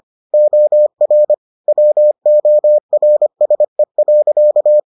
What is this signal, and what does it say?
Morse code, heard as a keyed tone, about 25 wpm, 600 Hz. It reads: OR WORSE.